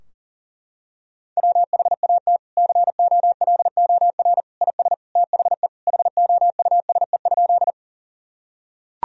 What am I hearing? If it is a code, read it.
Morse code, heard as a keyed tone, about 40 wpm, 700 Hz. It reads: WHAT COLOR IS THE HOUSE?
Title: Sentences from Top 1000 Words - Review Part 2 - 40wpm